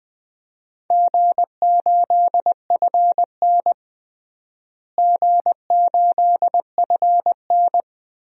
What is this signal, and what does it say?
Morse code, heard as a keyed tone, about 20 wpm, 700 Hz. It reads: G8FN G8FN
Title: DXCC-100 Call Signs - 4 Characters - Repeated 3x - 20wpm